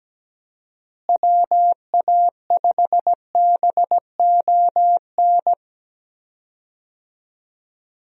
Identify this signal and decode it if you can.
Morse code, heard as a keyed tone, about 17 wpm, 700 Hz. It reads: WA5BON
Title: USA Call Signs - 6 Characters - Repeated 2x - 17wpm